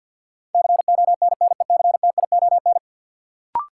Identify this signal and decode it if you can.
Morse code, heard as a keyed tone, about 53 wpm, 700 Hz. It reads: CONNECTION
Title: Sentences from Top 1300 Words - New Word Drill Alt - 50wpm